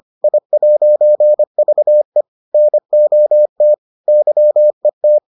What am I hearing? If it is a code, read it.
Morse code, heard as a keyed tone, about 25 wpm, 600 Hz. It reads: I'VE NOT YET